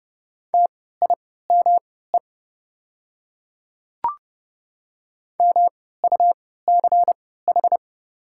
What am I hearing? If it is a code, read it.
Morse code, heard as a keyed tone, about 30 wpm, spaced as for 17 wpm, 690 Hz. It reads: TIME MUCH
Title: 3x Character Spacing - Top 300 Words - 30wpm